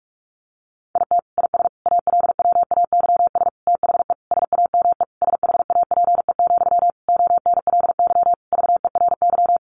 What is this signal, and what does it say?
Morse code, heard as a keyed tone, about 44 wpm, 700 Hz. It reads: IT IS ALWAYS THE SAME SHAPE, ONLY VERY
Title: Sentences from Top 1400 Words 45wpm